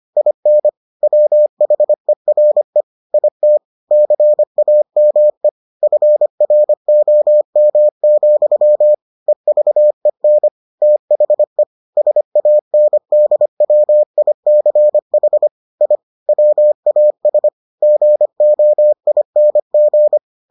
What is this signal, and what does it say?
Morse code, heard as a keyed tone, about 25 wpm, 600 Hz. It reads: IN WHERE IT CAME FROM, EVEN THE SANDWICH I WAS GOING